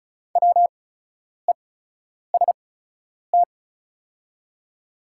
Morse code, heard as a keyed tone, about 36 wpm, spaced as for 10 wpm, 700 Hz. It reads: WEST
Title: 8x Character Spacing - Top 500 Words - 35wpm